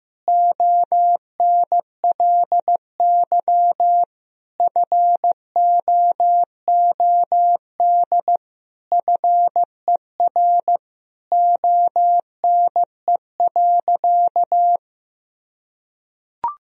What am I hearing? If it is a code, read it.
Morse code, heard as a keyed tone, about 15 wpm, 700 Hz. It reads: ONLY FOOD FER ONE.
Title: Sentences from Top 300 Words and Abbreviations 15wpm